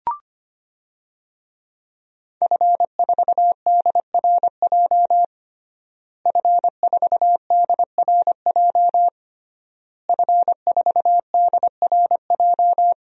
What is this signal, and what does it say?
Morse code, heard as a keyed tone, about 25 wpm, 700 Hz. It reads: F4DRJ F4DRJ F4DRJ
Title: DXCC-100 Call Signs - 5 Characters - Repeated 3x - 25wpm